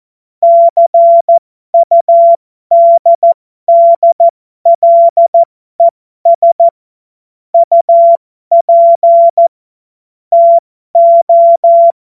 Morse code, heard as a keyed tone, about 14 wpm, spaced as for 12 wpm, 680 Hz. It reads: CUDDLES UP TO